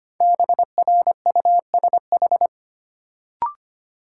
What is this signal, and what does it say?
Morse code, heard as a keyed tone, about 25 wpm, 700 Hz. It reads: BRUSH
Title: Sentences from Top 1100 Words - New Word Drill Alt - 25wpm